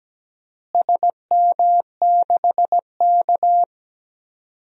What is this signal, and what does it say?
Morse code, heard as a keyed tone, about 17 wpm, 700 Hz. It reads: SM6K
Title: DXCC-100 Call Signs - 4 Characters - Repeated 2x - 17wpm